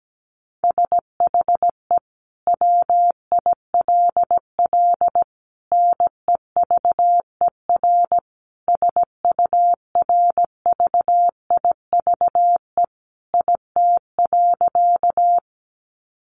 Morse code, heard as a keyed tone, about 17 wpm, 700 Hz. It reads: SHE WILL NEVER SURVIVE IT.
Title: Sentences from Top 1300 Words 17wpm